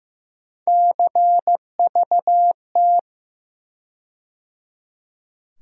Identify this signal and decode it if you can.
Morse code, heard as a keyed tone, about 15 wpm, 700 Hz. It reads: CVT
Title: USA Call Sign Postfixes - 3 Letters 15wpm